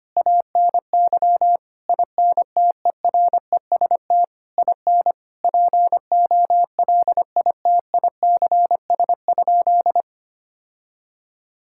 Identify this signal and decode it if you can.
Morse code, heard as a keyed tone, about 25 wpm, 700 Hz. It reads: ANY INTEREST IN POLITICS?
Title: Sentences from Top 1000 Words - Review Part 6 - 25wpm